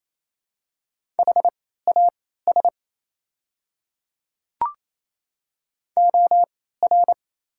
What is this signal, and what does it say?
Morse code, heard as a keyed tone, about 28 wpm, spaced as for 16 wpm, 700 Hz. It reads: HAS OR
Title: 3x Character Spacing - Top 300 Words - 28wpm